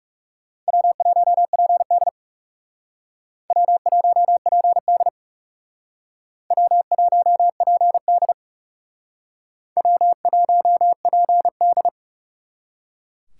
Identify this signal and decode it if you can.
Morse code, heard as a keyed tone, about 40 wpm, 700 Hz. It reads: W1PD W1PD W1PD W1PD